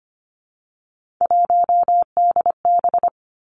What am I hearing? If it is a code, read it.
Morse code, heard as a keyed tone, about 25 wpm, 700 Hz. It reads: 1DB
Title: Three Challenging Letter-Number Combinations 25wpm